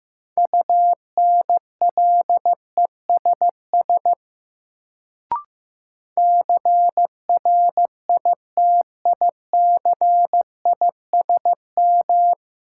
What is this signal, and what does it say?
Morse code, heard as a keyed tone, about 15 wpm, 700 Hz. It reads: UNLESS CRITICISM
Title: Sentences from Top 1300 Words - New Word Drill Alt - 15wpm